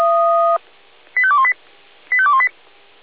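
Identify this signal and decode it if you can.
Five-tone selective call (ZVEI), heard as quick runs of five tones, tones of about 70 ms. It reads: 85218 85218
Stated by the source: ZVEI five-tone call 85280 received by SDR.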